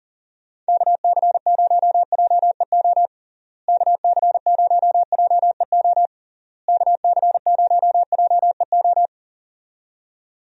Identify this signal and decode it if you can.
Morse code, heard as a keyed tone, about 40 wpm, 700 Hz. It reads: KC0JEO KC0JEO KC0JEO
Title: USA Call Signs - 6 Characters - Repeated 3x - 40wpm